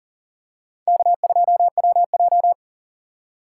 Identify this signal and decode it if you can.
Morse code, heard as a keyed tone, about 40 wpm, 700 Hz. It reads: K2WJ
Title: USA Call Signs - 4 Characters - 6x Character Spacing - 40wpm